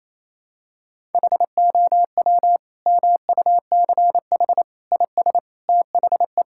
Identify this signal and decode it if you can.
Morse code, heard as a keyed tone, about 28 wpm, 700 Hz. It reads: HOW MUCH IS THE
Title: Sentences from Top 1000 Words - Review Part 5 - 28wpm